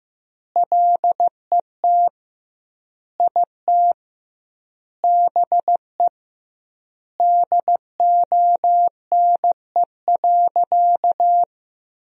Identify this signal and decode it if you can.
Morse code, heard as a keyed tone, about 15 wpm, 700 Hz. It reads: LET IT BE DONE.
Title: 2x Word-Spacing - Sentences from Top 500 Words - 15wpm